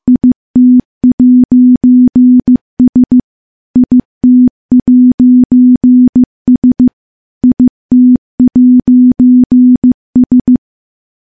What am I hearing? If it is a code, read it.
Morse code, heard as a keyed tone, about 15 wpm, 250 Hz. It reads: IT'S IT'S IT'S
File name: morse-training-15WPM